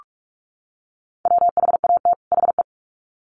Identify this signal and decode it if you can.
Morse code, heard as a keyed tone, about 44 wpm, 700 Hz. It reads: WHAT HE
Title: Sets of 2 Words 45wpm